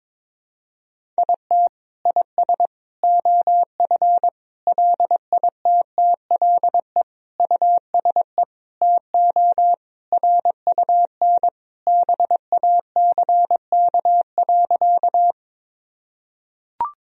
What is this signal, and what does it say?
Morse code, heard as a keyed tone, about 22 wpm, 700 Hz. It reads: IT IS OF LITTLE USE TO RUN BACK.
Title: Sentences from Top 1000 Words - Review Part 1 - 22wpm